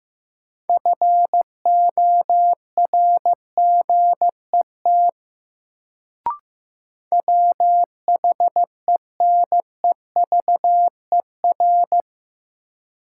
Morse code, heard as a keyed tone, about 15 wpm, 700 Hz. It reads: FORGET WHENEVER
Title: Sentences from Top 1200 Words - New Word Drill - 15wpm